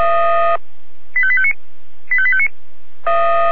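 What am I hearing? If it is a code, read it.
Five-tone selective call (ZVEI), heard as quick runs of five tones, tones of about 70 ms. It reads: 85859 85859
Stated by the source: ZVEI five-tone selective call 85820 received on SDR.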